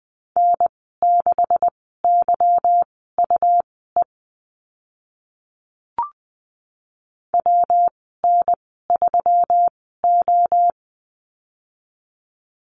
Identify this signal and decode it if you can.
Morse code, heard as a keyed tone, about 20 wpm, spaced as for 15 wpm, 700 Hz. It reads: N6YUE WN3O
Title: 147 Lesson 24g - Letter y - Call Signs 20wpm